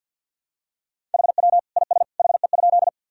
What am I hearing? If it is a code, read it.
Morse code, heard as a keyed tone, about 51 wpm, 700 Hz. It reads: HW IS HE?
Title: Sentences from Top 100 Words and Abbreviations 50wpm